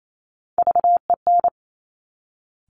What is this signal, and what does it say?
Morse code, heard as a keyed tone, about 28 wpm, 700 Hz. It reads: VEN
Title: Top Three Letter Combinations in Words 28wpm